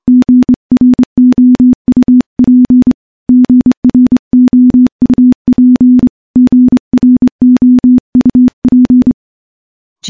Morse code, heard as a keyed tone, about 25 wpm, 250 Hz. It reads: GROUP GROUP GROUP
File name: morse-training-25WPM